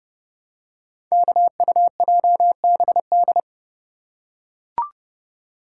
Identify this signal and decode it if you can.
Morse code, heard as a keyed tone, about 30 wpm, 700 Hz. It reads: KUJBD
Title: Code Groups - 5 Letter:Numbers - 30wpm